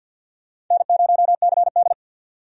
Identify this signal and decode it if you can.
Morse code, heard as a keyed tone, about 50 wpm, 700 Hz. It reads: N0CD
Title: USA Call Signs - 4 Characters - 50wpm